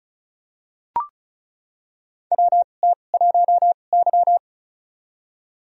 Morse code, heard as a keyed tone, about 34 wpm, spaced as for 25 wpm, 700 Hz. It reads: WT1Y
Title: USA Call Signs - 4 Characters - 2x Character Spacing - 35wpm